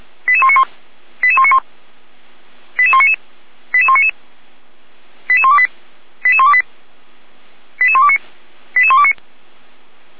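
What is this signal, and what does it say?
Five-tone selective call (ZVEI), heard as quick runs of five tones, tones of about 70 ms. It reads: 88191 88191 88199 88199 88128 88128 88129 88129
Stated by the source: ZVEI five-tone call received by SDR, fire brigade feuerwehr-orlamuende (F85360).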